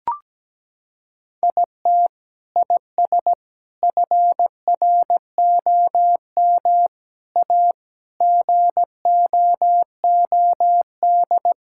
Morse code, heard as a keyed tone, about 17 wpm, 700 Hz. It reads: IT IS FROM A GOOD
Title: Sentences from Top 1000 Words - Review Part 5 - 17wpm